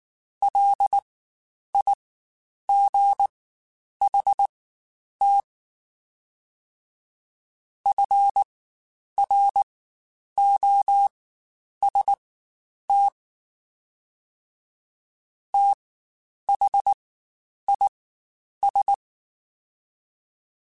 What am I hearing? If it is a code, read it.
Morse code, heard as a keyed tone, about 19 wpm, spaced as for 9 wpm, 800 Hz. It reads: LIGHT FROST THIS
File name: Spaced Characters / Spaced Characters-mp3s / Weather 4 24 sp @18